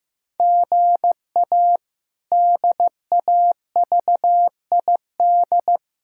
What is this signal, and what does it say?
Morse code, heard as a keyed tone, about 16 wpm, 700 Hz. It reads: GA DAVID
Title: K1USN SST QSO Practice - 15wpm